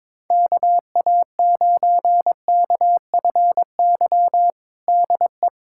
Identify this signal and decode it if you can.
Morse code, heard as a keyed tone, about 22 wpm, 700 Hz. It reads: KA9KFY DE
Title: USA Callsign DE Callsign - 6 Characters - 22wpm